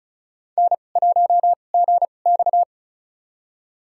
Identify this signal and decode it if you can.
Morse code, heard as a keyed tone, about 36 wpm, spaced as for 25 wpm, 700 Hz. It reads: N1GX
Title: USA Call Signs - 4 Characters - 2x Character Spacing - 35wpm